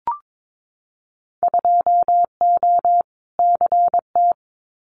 Morse code, heard as a keyed tone, about 22 wpm, 700 Hz. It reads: 2O CT
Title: Winter Field Day Exchanges 22wpm